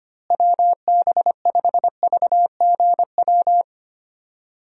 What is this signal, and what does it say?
Morse code, heard as a keyed tone, about 25 wpm, 700 Hz. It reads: WB5VGW
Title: USA Call Signs - 6 Characters - Repeated 3x - 25wpm